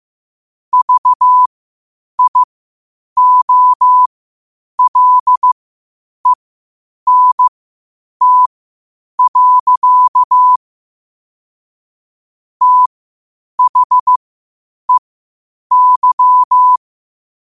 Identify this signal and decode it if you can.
Morse code, heard as a keyed tone, about 15 wpm, spaced as for 8 wpm, 1000 Hz. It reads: VIOLENT. THEY
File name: qotdfiles / 8wpm